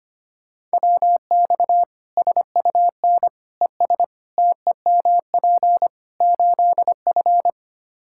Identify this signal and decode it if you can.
Morse code, heard as a keyed tone, about 25 wpm, 700 Hz. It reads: WX SUN ES TEMP 8F
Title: Practice QSO - Exchange 2 - Info - 25wpm